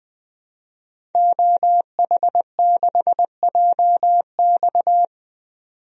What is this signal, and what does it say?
Morse code, heard as a keyed tone, about 20 wpm, 700 Hz. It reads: OH6JX
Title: DXCC-100 Call Signs - 5 Characters - 20wpm